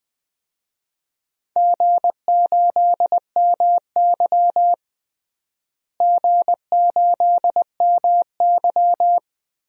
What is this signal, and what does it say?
Morse code, heard as a keyed tone, about 21 wpm, 700 Hz. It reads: G8MY G8MY
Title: DXCC-100 Call Signs - 4 Characters - Repeated 3x - 20wpm